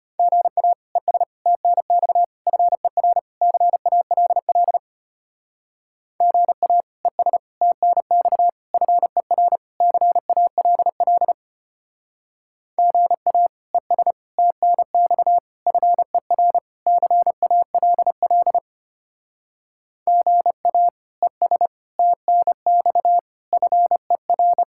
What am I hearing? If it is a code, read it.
Morse code, heard as a keyed tone, about 32 wpm, 700 Hz. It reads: GA ES TNX FER CALL GA ES TNX FER CALL GA ES TNX FER CALL GA ES TNX FER